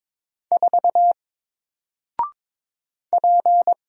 Morse code, heard as a keyed tone, about 22 wpm, 700 Hz. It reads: 4 P
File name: Single-Letter-Number-22wpm